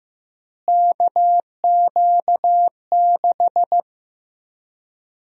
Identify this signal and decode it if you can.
Morse code, heard as a keyed tone, about 15 wpm, 700 Hz. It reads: KQ6
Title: USA Call Sign Prefixes - Repeated 2x - 15wpm